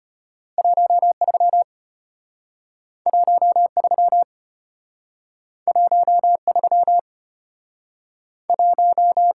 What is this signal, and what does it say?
Morse code, heard as a keyed tone, about 33 wpm, 690 Hz. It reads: 13 13 13 1